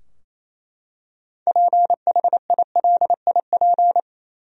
Morse code, heard as a keyed tone, about 28 wpm, 700 Hz. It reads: PHILIP
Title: USA Common Names - 28wpm